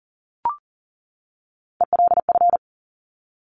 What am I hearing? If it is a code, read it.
Morse code, heard as a keyed tone, about 39 wpm, 700 Hz. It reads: ELF
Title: Top Three Letter Combinations in Words 40wpm